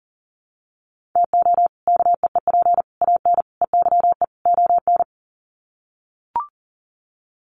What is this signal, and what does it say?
Morse code, heard as a keyed tone, about 40 wpm, 700 Hz. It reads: TO KEEP AN EYE ON
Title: Sets of 5 Words Encore 40wpm